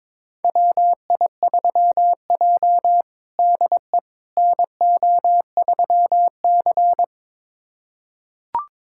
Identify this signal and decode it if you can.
Morse code, heard as a keyed tone, about 22 wpm, 700 Hz. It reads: WI3J DE NO3C
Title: USA Callsign DE Callsign - 4 Characters - 22wpm